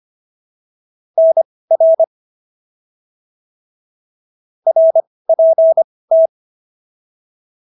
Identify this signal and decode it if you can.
Morse code, heard as a keyed tone, about 25 wpm, spaced as for 18 wpm, 650 Hz. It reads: NR RPT